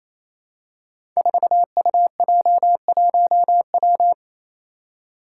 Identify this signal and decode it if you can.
Morse code, heard as a keyed tone, about 27 wpm, 700 Hz. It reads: 4UJ1W
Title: Five Challenging Letter-Number Combinations 28wpm